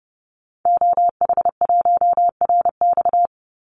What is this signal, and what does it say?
Morse code, heard as a keyed tone, about 30 wpm, 700 Hz. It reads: OH1RX
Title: Call Signs - 30wpm